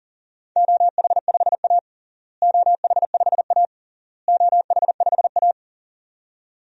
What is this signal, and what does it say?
Morse code, heard as a keyed tone, about 40 wpm, 700 Hz. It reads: OH5A OH5A OH5A
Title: DXCC-100 Call Signs - 4 Characters - Repeated 3x - 40wpm